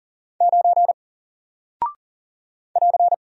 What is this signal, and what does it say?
Morse code, heard as a keyed tone, about 40 wpm, 700 Hz. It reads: 9 <AR>
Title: Single Character (Letter:Number:Punctuation:Prosign) Alt 40wpm